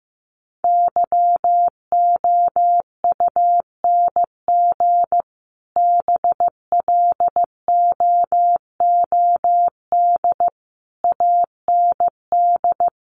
Morse code, heard as a keyed tone, about 15 wpm, 700 Hz. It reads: YOUNG BLOOD AND
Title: Sentences from Top 1100 Words 15wpm